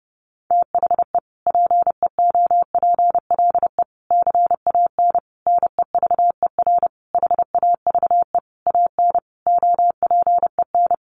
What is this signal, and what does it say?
Morse code, heard as a keyed tone, about 28 wpm, 700 Hz. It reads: THE PEOPLE CAN NEVER HAVE AN OPEN
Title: Sentences from Top 1400 Words 30wpm